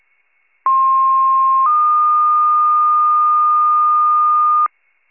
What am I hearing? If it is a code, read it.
Two-tone page, heard as a one-second tone, then a three-second tone, about 1060 then 1220 Hz, with noise. It is heard from 0.7 to 4.7 s.
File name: wcalert